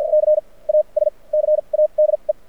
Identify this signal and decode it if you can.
Morse code, heard as a keyed tone, about 57 wpm, spaced as for 33 wpm, 590 Hz. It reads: CASCADE
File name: archived / CRSnet